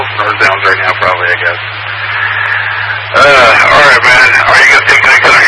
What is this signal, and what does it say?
Emergency Alert System (SAME) tones, heard as harsh buzzing bursts, with speech. It is heard from 1.2 to 4.4 s.